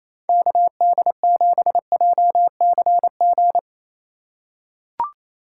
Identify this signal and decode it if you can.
Morse code, heard as a keyed tone, about 28 wpm, 700 Hz. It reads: KD7JCG
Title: USA Call Signs - 2x Character Spacing - 28wpm